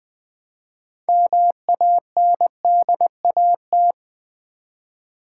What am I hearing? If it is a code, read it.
Morse code, heard as a keyed tone, about 20 wpm, 700 Hz. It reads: MANDAT
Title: Build A Word - 3 Letters At A Time - 9-Letter Words - 20wpm